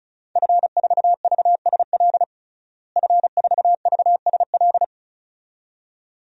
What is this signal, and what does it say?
Morse code, heard as a keyed tone, about 34 wpm, 700 Hz. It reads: F4VSL F4VSL
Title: DXCC-100 Call Signs - 5 Characters - Repeated 2x - 35wpm